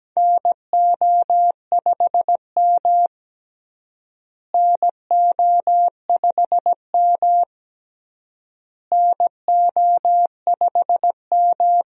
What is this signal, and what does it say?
Morse code, heard as a keyed tone, about 17 wpm, 700 Hz. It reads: NO5M NO5M NO5M
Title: USA Call Signs - 4 Characters - Repeated 3x - 17wpm